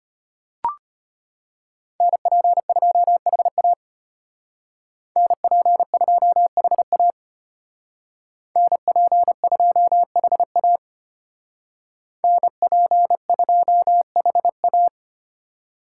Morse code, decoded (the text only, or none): NP2HA NP2HA NP2HA NP2HA